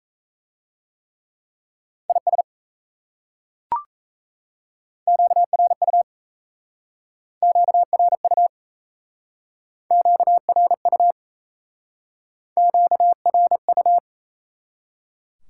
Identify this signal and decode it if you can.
Morse code, heard as a keyed tone, about 37 wpm, 700 Hz. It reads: IS QRU QRU QRU QRU